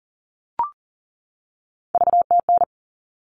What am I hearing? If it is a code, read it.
Morse code, heard as a keyed tone, about 40 wpm, 700 Hz. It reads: VTN
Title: All Cut Numbers - 3-Digit 40wpm